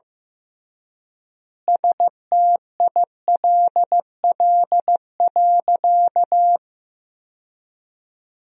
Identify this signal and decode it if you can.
Morse code, heard as a keyed tone, about 15 wpm, 700 Hz. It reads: STILL.
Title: 3x Word-Spacing - Sentences from Top 500 Words - 15wpm